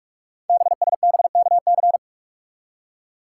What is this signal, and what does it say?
Morse code, heard as a keyed tone, about 44 wpm, 700 Hz. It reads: BSBKC
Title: Code Groups - 5 Letters - 45wpm